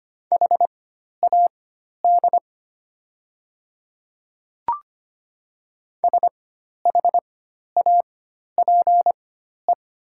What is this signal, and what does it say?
Morse code, heard as a keyed tone, about 25 wpm, spaced as for 12 wpm, 700 Hz. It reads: HAD SHAPE
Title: 4x Character Spacing - Top 500 Words - 25wpm